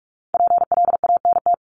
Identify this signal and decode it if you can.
Morse code, heard as a keyed tone, about 44 wpm, 700 Hz. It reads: PLANT
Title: Top 300 Words 45wpm